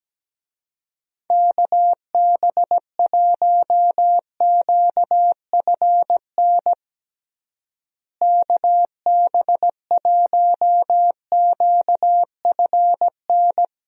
Morse code, heard as a keyed tone, about 17 wpm, 700 Hz. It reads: KB1QFN KB1QFN
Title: USA Call Signs - 6 Characters - Repeated 2x - 17wpm